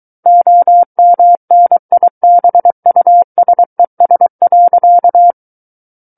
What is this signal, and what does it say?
Morse code, heard as a keyed tone, about 23 wpm, 700 Hz. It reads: OMNIBUSES.
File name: Book One / 14-Chapter 14 - 23 WPM